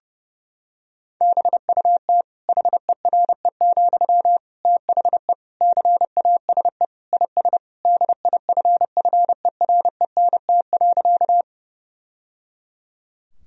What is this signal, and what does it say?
Morse code, heard as a keyed tone, about 30 wpm, 700 Hz. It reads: BUT HERE, THE CASE IS DIFFERENT.
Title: Sentences from Top 1000 Words - Review Part 3 - 30wpm